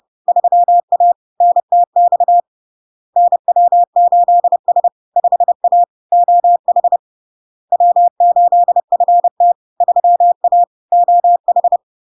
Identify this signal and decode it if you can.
Morse code, heard as a keyed tone, about 30 wpm, 700 Hz. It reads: 3A NTX NW8S 5A OH W8FT 3A OH
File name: LARK-FD-2022-Log-in-CW-30wpm-700-Hz